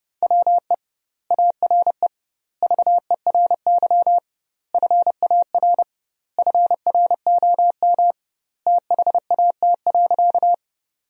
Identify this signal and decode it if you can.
Morse code, heard as a keyed tone, about 30 wpm, 700 Hz. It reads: WE ARE VERY FAR FROM THAT.
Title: 2x Word-Spacing - Sentences from Top 300 Words - 30wpm